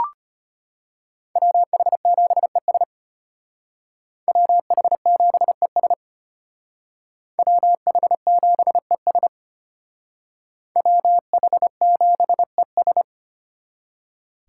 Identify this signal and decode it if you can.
Morse code, heard as a keyed tone, about 33 wpm, 700 Hz. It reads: WH7ES WH7ES WH7ES WH7ES